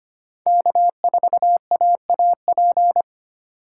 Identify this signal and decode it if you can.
Morse code, heard as a keyed tone, about 25 wpm, 700 Hz. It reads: K4AAP
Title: USA Call Signs - 5 Characters - 2x Character Spacing - 25wpm